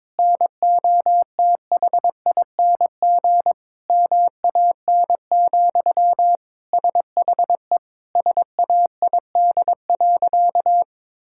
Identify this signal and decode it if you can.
Morse code, heard as a keyed tone, about 22 wpm, 700 Hz. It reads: NOTHING MAN, SHE SAID.
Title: Sentences from Top 1000 Words - Review Part 6 - 22wpm